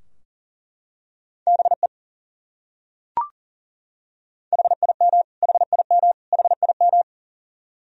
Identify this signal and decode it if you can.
Morse code, heard as a keyed tone, about 40 wpm, 700 Hz. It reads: BE HIM HIM HIM
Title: Top 100 Words Repeated 40wpm